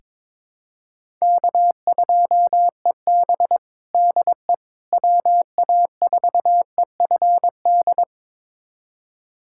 Morse code, heard as a keyed tone, about 21 wpm, 700 Hz. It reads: K2EB DE WA4EFD
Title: USA Callsign DE Callsign 22wpm